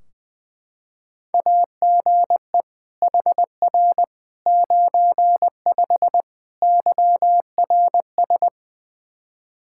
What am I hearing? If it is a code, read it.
Morse code, heard as a keyed tone, about 20 wpm, 700 Hz. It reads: AGE HR 95 YRS
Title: Practice QSO - Exchange 3 - More Info - 20wpm